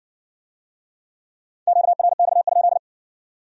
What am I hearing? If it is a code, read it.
Morse code, heard as a keyed tone, about 56 wpm, 700 Hz. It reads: QRX?